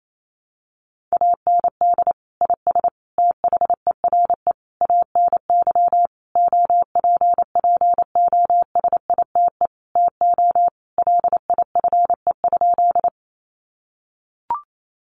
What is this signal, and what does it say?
Morse code, heard as a keyed tone, about 28 wpm, 700 Hz. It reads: AND IS THERE ANY OPPOSITE TO LIFE?
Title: Sentences from Top 1300 Words 28wpm